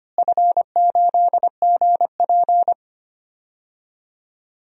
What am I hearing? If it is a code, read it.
Morse code, heard as a keyed tone, about 25 wpm, 700 Hz. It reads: F8GP
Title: DXCC-100 Call Signs - 4 Characters - Repeated 2x - 25wpm